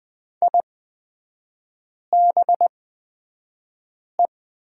Morse code, heard as a keyed tone, about 20 wpm, spaced as for 5 wpm, 700 Hz. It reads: IBE
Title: 250 Lesson 40e - BK - Three Characters 20wpm